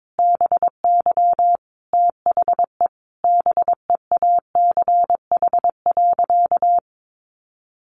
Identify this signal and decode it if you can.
Morse code, heard as a keyed tone, about 22 wpm, 700 Hz. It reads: BY THE BEACH.
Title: Sentences from Top 1500 Words 22wpm